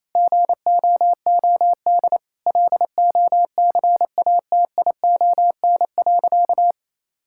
Morse code, heard as a keyed tone, about 27 wpm, 690 Hz. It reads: GOOD LOCATION.
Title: Sentences from Top 1000 Words - Review Part 6 - 28wpm